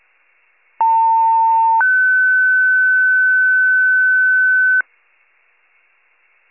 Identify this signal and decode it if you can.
Two-tone page, heard as a one-second tone, then a three-second tone, about 900 then 1500 Hz. It is heard from 0.8 to 4.8 s.
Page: 78 FIRE